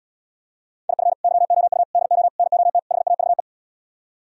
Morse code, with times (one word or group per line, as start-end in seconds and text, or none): 0.9-1.1 IF
1.2-1.8 YOU
1.9-2.3 DO
2.4-2.8 NOT
2.9-3.4 HAVE